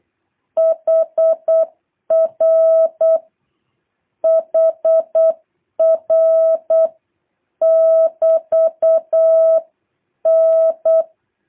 Morse code, heard as a keyed tone, about 8 wpm, 660 Hz. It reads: HR HR <BT> N
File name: jh8cbh / 2023cw